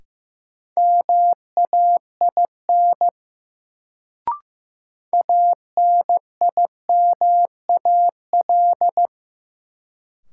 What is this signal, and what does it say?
Morse code, heard as a keyed tone, about 16 wpm, 700 Hz. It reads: MAIN ANIMAL
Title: Sentences from Top 1300 Words - New Word Drill Alt - 15wpm